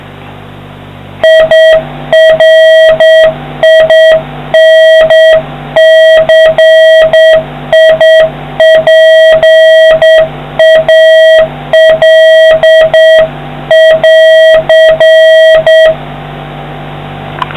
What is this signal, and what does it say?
Morse code, heard as a keyed tone, about 8 wpm, 3200 Hz, with noise. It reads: IRINCIPAL<AR>